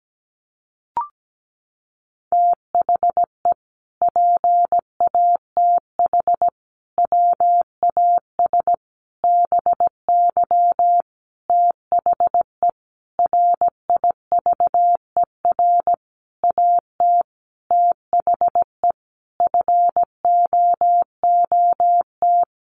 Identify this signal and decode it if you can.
Morse code, heard as a keyed tone, about 17 wpm, 700 Hz. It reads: THE PATH WAS BY THE RIVER AT THE FOOT
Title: Sentences from Top 1400 Words 17wpm